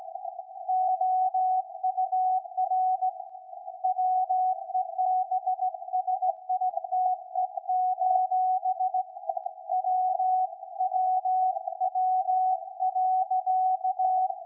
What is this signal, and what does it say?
Morse code, heard as a keyed tone, about 17 wpm, 740 Hz. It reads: OUR WEBSITE: WWW.
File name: iz7slz / VLF / SAQ 13-02-2023